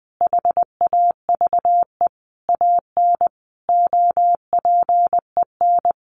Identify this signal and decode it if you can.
Morse code, heard as a keyed tone, about 19 wpm, 700 Hz. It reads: HAVE AN OPEN
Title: Sentences from Top 1400 Words 20wpm